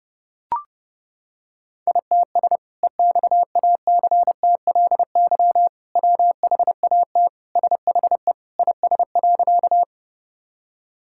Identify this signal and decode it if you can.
Morse code, heard as a keyed tone, about 30 wpm, 700 Hz. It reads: ITS EXACTLY WHAT SHE IS.
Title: Sentences from Top 1000 Words - Review Part 6 - 30wpm